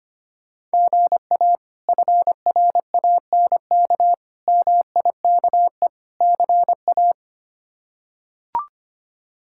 Morse code, heard as a keyed tone, about 25 wpm, 700 Hz. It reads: GA FRANK MIKE CA
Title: K1USN SST QSO Practice - 25wpm